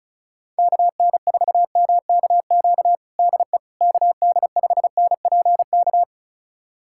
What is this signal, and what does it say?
Morse code, heard as a keyed tone, about 34 wpm, 700 Hz. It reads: KN4MKQ DE KD5NPK
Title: USA Callsign DE Callsign - 6 Characters - 35wpm